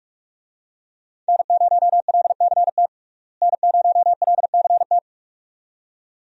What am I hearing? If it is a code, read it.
Morse code, heard as a keyed tone, about 43 wpm, 700 Hz. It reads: N0LCT N0LCT
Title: USA Call Signs - 5 Characters - Repeated 2x - 45wpm